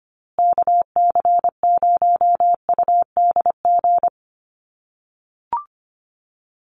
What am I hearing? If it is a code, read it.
Morse code, heard as a keyed tone, about 25 wpm, 700 Hz. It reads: KC0UDG
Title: USA Call Signs Repeated 3x 25wpm